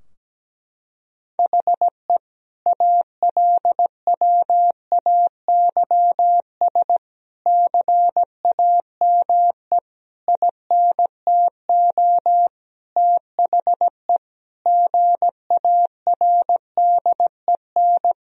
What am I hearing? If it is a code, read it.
Morse code, heard as a keyed tone, about 17 wpm, 700 Hz. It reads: HE ALWAYS CAME INTO THE GARDEN
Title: Sentences from Top 1000 Words - Review Part 4 - 17wpm